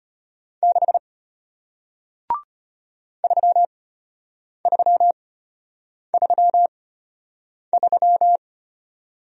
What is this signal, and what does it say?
Morse code, heard as a keyed tone, about 33 wpm, spaced as for 8 wpm, 700 Hz. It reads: 6 3333